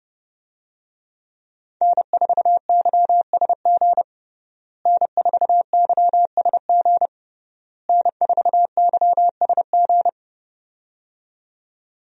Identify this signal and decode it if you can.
Morse code, heard as a keyed tone, about 30 wpm, 700 Hz. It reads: N4YSG N4YSG N4YSG
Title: USA Call Signs - 5 Characters - Repeated 3x - 30wpm